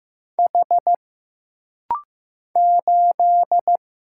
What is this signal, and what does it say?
Morse code, heard as a keyed tone, about 15 wpm, 700 Hz. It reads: H 8
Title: Single Letter-Number Alt - Rapid-Fire 15wpm